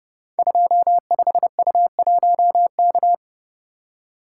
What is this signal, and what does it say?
Morse code, heard as a keyed tone, about 28 wpm, 700 Hz. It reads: 25U1K
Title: Code Groups - 5 Letter:Numbers - 30wpm